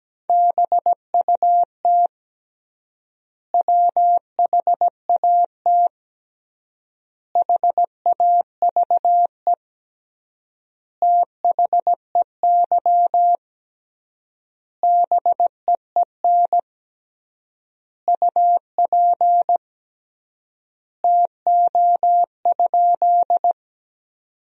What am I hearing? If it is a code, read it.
Morse code, heard as a keyed tone, about 17 wpm, 700 Hz. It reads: BUT WHAT HAVE THEY BEEN UP TO?
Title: 3x Word-Spacing - Sentences from Top 100 Words - 17wpm